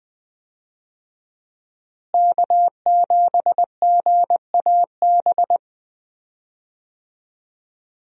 Morse code, heard as a keyed tone, about 20 wpm, 700 Hz. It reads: K7GAB
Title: USA Call Signs - 5 Characters - Repeated 2x - 20wpm